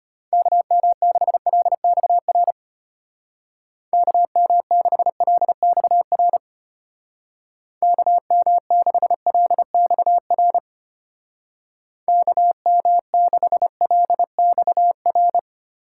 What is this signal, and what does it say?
Morse code, heard as a keyed tone, about 32 wpm, 700 Hz. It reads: KM6LXR KM6LXR KM6LXR KM6LXR